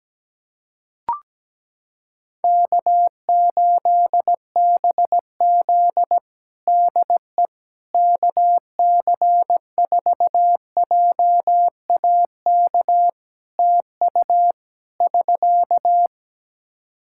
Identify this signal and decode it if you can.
Morse code, heard as a keyed tone, about 17 wpm, 700 Hz. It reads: K8BZ DE KC4JAK TU <SK>
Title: Practice QSO - Exchange 4 - Ending - 17wpm